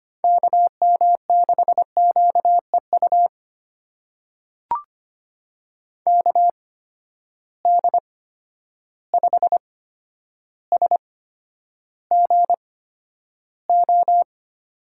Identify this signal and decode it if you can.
Morse code, heard as a keyed tone, about 25 wpm, spaced as for 11 wpm, 700 Hz. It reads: KM6QEU KD5SGO
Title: USA Call Signs - 6 Characters - 8x Character Spacing - 25wpm